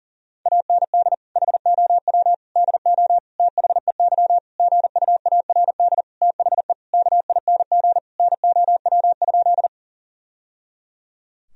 Morse code, heard as a keyed tone, about 40 wpm, 700 Hz. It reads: AND HOW DO THEY GUARD THE KING NOW?
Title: Sentences from Top 1000 Words - Review Part 4 - 40wpm